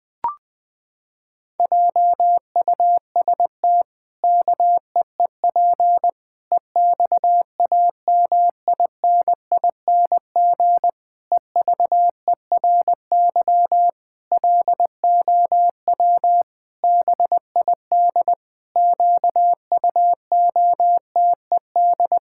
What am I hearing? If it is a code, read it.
Morse code, heard as a keyed tone, about 20 wpm, 700 Hz. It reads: JUST KEEP EXAMINING EVERY LOW BID QUOTED
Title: Pangram Challenge v2 - 20wpm